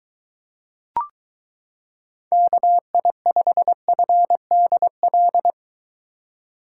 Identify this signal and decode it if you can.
Morse code, heard as a keyed tone, about 23 wpm, 700 Hz. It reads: KI5FDL